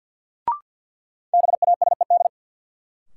Morse code, heard as a keyed tone, about 53 wpm, 700 Hz. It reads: BASED
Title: Sets of 2 Words - New Word Drill - 50wpm